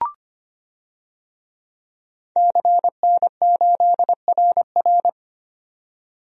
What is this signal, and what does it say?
Morse code, heard as a keyed tone, about 25 wpm, 700 Hz. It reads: CN8RR
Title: DXCC-100 Call Signs - 5 Characters - 25wpm